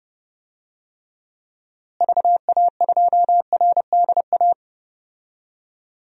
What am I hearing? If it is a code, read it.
Morse code, heard as a keyed tone, about 30 wpm, 700 Hz. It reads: VA2RDA
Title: DXCC-100 Call Signs - 6 Characters - Repeated 3x - 30wpm